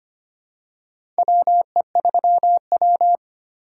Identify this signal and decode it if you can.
Morse code, heard as a keyed tone, about 25 wpm, 700 Hz. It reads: WE3W
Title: USA Call Signs - 4 Characters - 2x Character Spacing - 25wpm